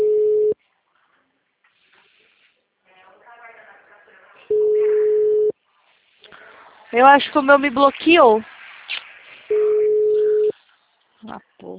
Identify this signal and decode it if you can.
Ringback tone (Europe, 425 Hz), with speech.